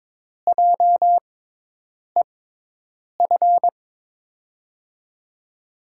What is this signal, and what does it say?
Morse code, heard as a keyed tone, about 22 wpm, spaced as for 8 wpm, 700 Hz. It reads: JEF
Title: USA Call Signs - 6x Character Spacing - 22wpm